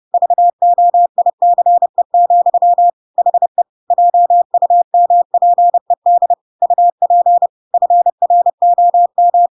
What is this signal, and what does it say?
Morse code, heard as a keyed tone, about 30 wpm, 700 Hz. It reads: VOICE, HE JUMPED UP FROM
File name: Pinocchio / Pinocchio ch7 30 30 ik1hge